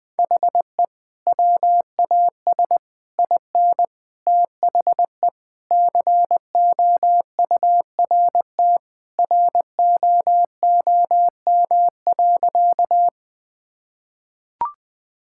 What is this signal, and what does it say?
Morse code, heard as a keyed tone, about 20 wpm, 700 Hz. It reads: HE WAS IN THE COURT ROOM.
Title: Sentences from Top 1600 Words 20wpm